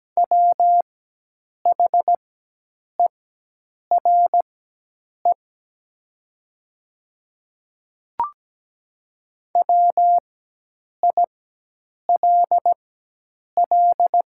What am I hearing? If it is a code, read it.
Morse code, heard as a keyed tone, about 17 wpm, spaced as for 8 wpm, 700 Hz. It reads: WHERE WILL
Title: 4x Character Spacing - Top 500 Words - 17wpm